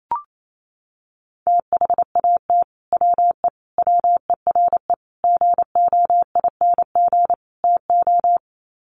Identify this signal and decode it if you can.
Morse code, heard as a keyed tone, about 28 wpm, 700 Hz. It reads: THAT WE WERE GOING TO
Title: Sets of 5 Words Encore2 28wpm